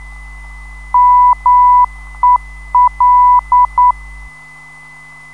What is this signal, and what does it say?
Morse code, heard as a keyed tone, about 9 wpm, 990 Hz. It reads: MEL
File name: ndb zvok / Mel